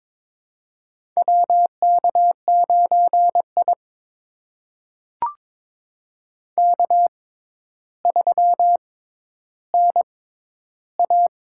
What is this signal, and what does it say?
Morse code, heard as a keyed tone, about 22 wpm, spaced as for 11 wpm, 700 Hz. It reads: WK9I K3NA
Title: USA Call Signs - 4 Characters - 6x Character Spacing - 22wpm